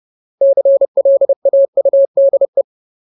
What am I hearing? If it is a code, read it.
Morse code, heard as a keyed tone, about 30 wpm, 550 Hz. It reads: CLAUDE
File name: Ultimate Word List / 30WPM Ultimate